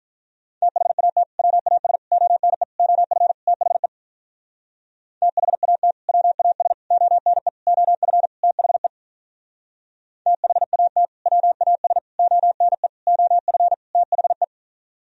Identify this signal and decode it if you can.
Morse code, heard as a keyed tone, about 43 wpm, 700 Hz. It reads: THAT WAS ONE OF THE THAT WAS ONE OF THE THAT WAS ONE OF THE
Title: Speed-Racing - Sets of 5 Words 35wpm